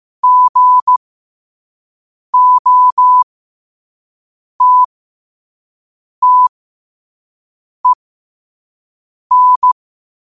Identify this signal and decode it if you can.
Morse code, heard as a keyed tone, about 15 wpm, spaced as for 5 wpm, 1000 Hz. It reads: GOTTEN
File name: qotdfiles / 5wpm